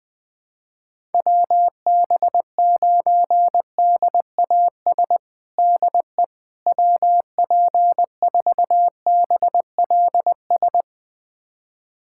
Morse code, heard as a keyed tone, about 20 wpm, 700 Hz. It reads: WB9DAS DE WP4BLS
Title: USA Callsign DE Callsign - 6 Characters - 20wpm